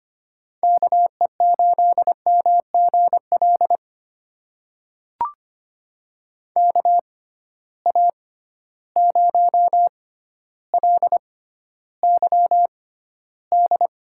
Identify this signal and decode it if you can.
Morse code, heard as a keyed tone, about 25 wpm, spaced as for 13 wpm, 700 Hz. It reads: KE8MGL KA0LYD
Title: USA Call Signs - 6 Characters - 6x Character Spacing - 25wpm